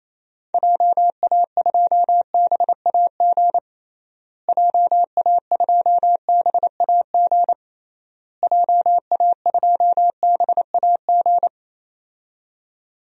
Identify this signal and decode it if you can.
Morse code, heard as a keyed tone, about 28 wpm, 700 Hz. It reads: JA2BAG JA2BAG JA2BAG
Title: DXCC-100 Call Signs - 6 Characters - Repeated 3x - 28wpm